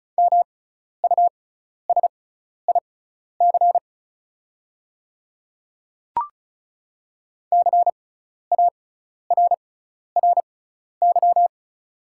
Morse code, heard as a keyed tone, about 36 wpm, spaced as for 12 wpm, 690 Hz. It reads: MUSIC CARRY
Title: 6x Character Spacing - Top 300 Words - 35wpm